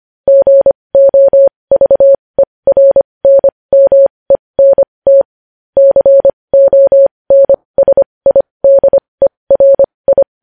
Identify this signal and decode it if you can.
Morse code, heard as a keyed tone, about 25 wpm, spaced as for 20 wpm, 550 Hz. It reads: GOVERNMENT CONSIDERI
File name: podcast / News headlines-1x-25x20